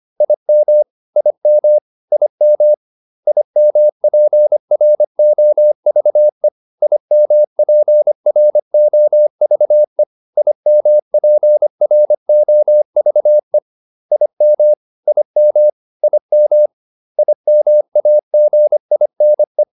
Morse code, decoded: IM IM IM IMPROVE IMPROVE IMPROVE IM IM IM IMAGINE

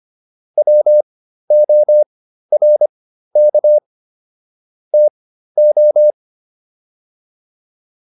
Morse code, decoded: WORK TO

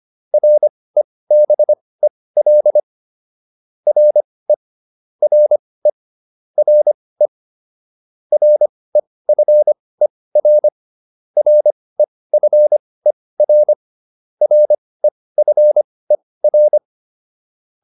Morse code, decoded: REBEL RE RE RE REFER REFER REFER

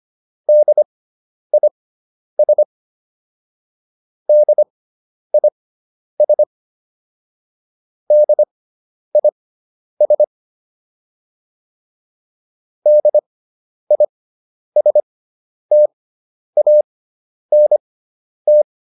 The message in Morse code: DIS DIS DIS DISTANT

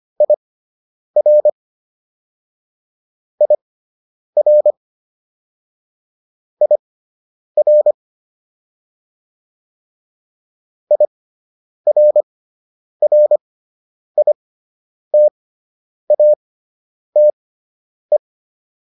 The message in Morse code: IR IR IR IRRITATE

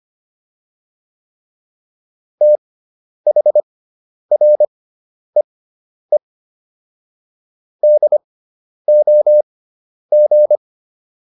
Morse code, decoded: THREE DOG

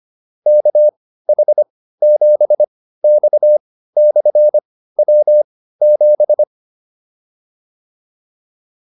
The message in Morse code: KH7X/W7

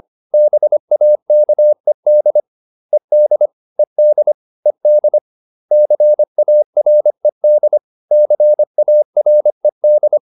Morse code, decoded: BAKED ED ED ED CARED CARED

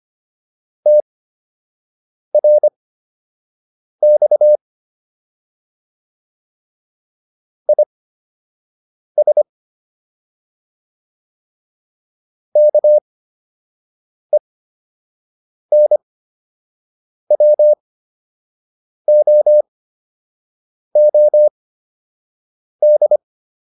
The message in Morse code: TRX IS KENWOOD

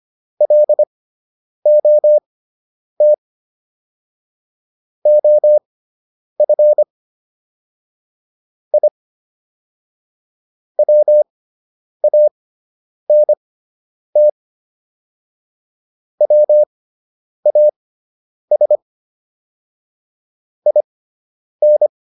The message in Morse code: LOT OF I WANT WAS IN

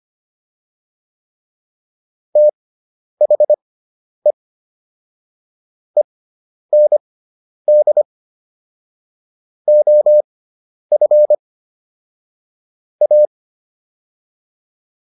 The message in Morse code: THE END OF A

